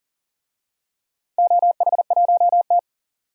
Morse code, heard as a keyed tone, about 37 wpm, 700 Hz. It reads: OH1T
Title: DXCC-100 Call Signs - 4 Characters - 40wpm